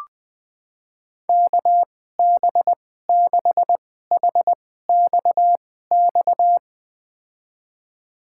Morse code, heard as a keyed tone, about 20 wpm, spaced as for 15 wpm, 700 Hz. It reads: KB6HXX
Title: USA Call Signs - 6 Characters - 2x Character Spacing - 20wpm